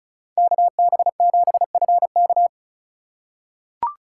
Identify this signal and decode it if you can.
Morse code, heard as a keyed tone, about 36 wpm, 700 Hz. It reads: KB7FK
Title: USA Call Signs - 5 Characters - 2x Character Spacing - 35wpm